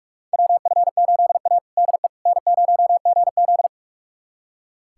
Morse code, heard as a keyed tone, about 46 wpm, 700 Hz. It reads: WF8A DE N0GZ
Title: USA Callsign DE Callsign - 4 Characters - 45wpm